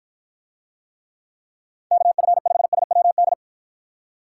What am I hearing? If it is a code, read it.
Morse code, heard as a keyed tone, about 53 wpm, 700 Hz. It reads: KF5SWD